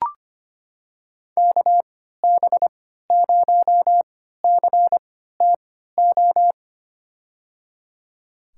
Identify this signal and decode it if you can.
Morse code, heard as a keyed tone, about 25 wpm, spaced as for 14 wpm, 700 Hz. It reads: KB0CTO
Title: USA Call Signs - 6 Characters - 3x Character Spacing - 25wpm